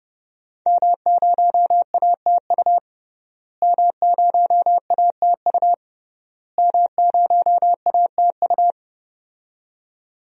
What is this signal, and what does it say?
Morse code, heard as a keyed tone, about 30 wpm, 700 Hz. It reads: M0ATU M0ATU M0ATU
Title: DXCC-100 Call Signs - 5 Characters - Repeated 3x - 30wpm